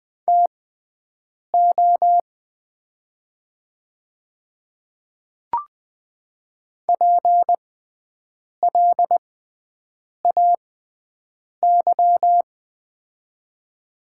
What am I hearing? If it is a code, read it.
Morse code, heard as a keyed tone, about 20 wpm, spaced as for 7 wpm, 690 Hz. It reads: TO PLAY
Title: 6x Character Spacing - Top 300 Words - 20wpm